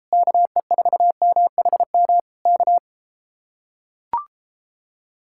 Morse code, heard as a keyed tone, about 33 wpm, 700 Hz. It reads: KE4MHM K